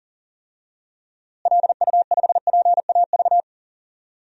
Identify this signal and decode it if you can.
Morse code, heard as a keyed tone, about 40 wpm, 700 Hz. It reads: LU5PAV
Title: DXCC-100 Call Signs - 6 Characters - Repeated 2x - 40wpm